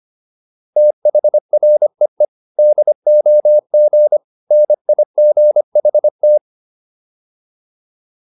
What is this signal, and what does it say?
Morse code, heard as a keyed tone, about 25 wpm, 600 Hz. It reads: THREE DOG NIGHT